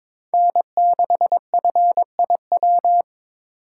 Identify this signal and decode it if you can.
Morse code, heard as a keyed tone, about 26 wpm, 700 Hz. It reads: N6FIW
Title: Speed Racing - USA 1x3 Call Signs 22wpm